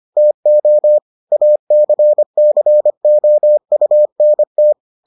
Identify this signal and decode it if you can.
Morse code, heard as a keyed tone, about 25 wpm, 600 Hz. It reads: TO ACCOUNT